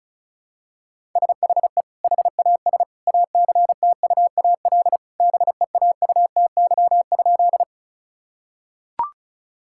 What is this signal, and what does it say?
Morse code, heard as a keyed tone, about 34 wpm, 700 Hz. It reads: SHE HAS ACTUAL BEAUTY?
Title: Sentences from Top 1000 Words - Review Part 5 - 35wpm